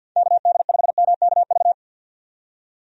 Morse code, heard as a keyed tone, about 50 wpm, 700 Hz. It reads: KD5MKV
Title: USA Call Signs - 6 Characters - 50wpm